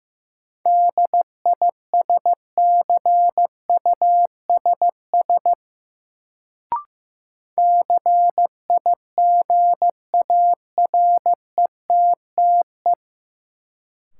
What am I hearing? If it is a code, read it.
Morse code, heard as a keyed tone, about 15 wpm, 700 Hz. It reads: DISCUSS CIGARETTE
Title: Sentences from Top 1000 Words - New Word Drill - 15wpm